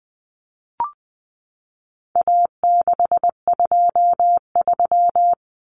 Morse code, heard as a keyed tone, about 20 wpm, 700 Hz. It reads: A623
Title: Common Cut Numbers - 4-Digit 20wpm